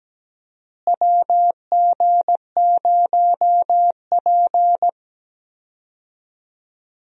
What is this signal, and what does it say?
Morse code, heard as a keyed tone, about 17 wpm, 700 Hz. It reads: WG0P
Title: USA Call Signs - 4 Characters - Repeated 3x - 17wpm